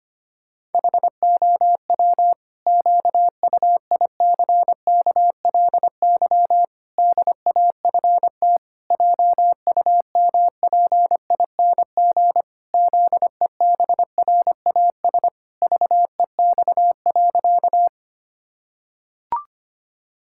Morse code, heard as a keyed tone, about 25 wpm, 700 Hz. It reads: HOW QUICKLY DAFT JUMPING ZEBRAS VEX.
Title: Pangram Challenge v2 - 25wpm